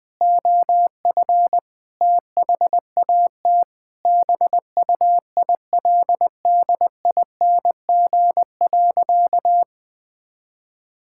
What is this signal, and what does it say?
Morse code, heard as a keyed tone, about 21 wpm, 700 Hz. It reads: OF THAT BUILDING.
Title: Sentences from Top 1000 Words - Review Part 5 - 20wpm